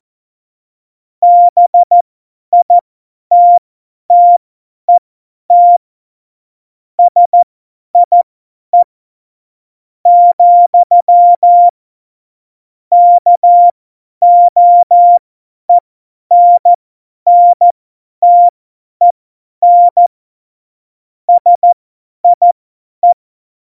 Morse code, heard as a keyed tone, about 14 wpm, spaced as for 10 wpm, 700 Hz. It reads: BITTET SIE , KOENNTEN SIE
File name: Morsekurs / Witze / Witz 15. September 2020, 50 F70